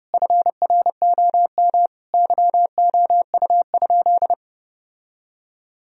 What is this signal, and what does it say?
Morse code, heard as a keyed tone, about 32 wpm, 700 Hz. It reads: FROM YOU?
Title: Sentences from Top 1000 Words - Review Part 1 - 30wpm